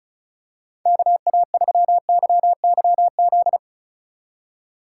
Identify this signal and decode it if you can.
Morse code, heard as a keyed tone, about 34 wpm, 700 Hz. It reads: KA3YYZ
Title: USA Call Signs - 4x Character Spacing - 35wpm